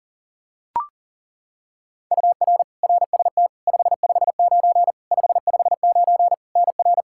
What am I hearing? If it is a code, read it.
Morse code, heard as a keyed tone, about 40 wpm, 700 Hz. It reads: UR RST 559 559 NR
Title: Practice QSO - Exchange 1 - Beginning - 40wpm